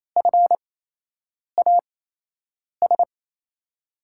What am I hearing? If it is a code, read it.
Morse code, heard as a keyed tone, about 28 wpm, spaced as for 8 wpm, 700 Hz. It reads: FAS
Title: USA Call Signs - 6 Characters - 8x Character Spacing - 28wpm